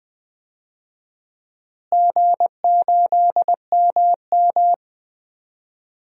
Morse code, heard as a keyed tone, about 20 wpm, 700 Hz. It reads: G8MM
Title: DXCC-100 Call Signs - 4 Characters - Repeated 2x - 20wpm